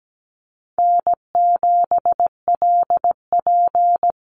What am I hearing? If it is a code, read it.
Morse code, heard as a keyed tone, about 17 wpm, 700 Hz. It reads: N7LP
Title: NA SOTA Chaser Callsigns 17wpm